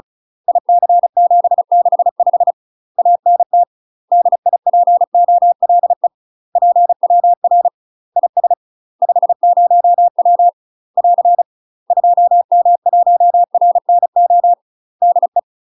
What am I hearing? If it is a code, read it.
Morse code, heard as a keyed tone, about 36 wpm, 700 Hz. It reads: IC765 ANT DIPOLE PWR IS 50W <AR> 2M1RNO DE
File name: UK MASTER QSOs TRAINING / 35WPM / UK MASTER QSOs 0010